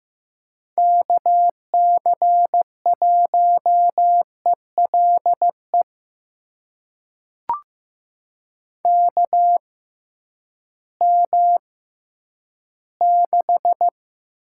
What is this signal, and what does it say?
Morse code, heard as a keyed tone, about 15 wpm, spaced as for 10 wpm, 700 Hz. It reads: KC1ELE KM6
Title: USA Call Signs - 6 Characters - 6x Character Spacing - 15wpm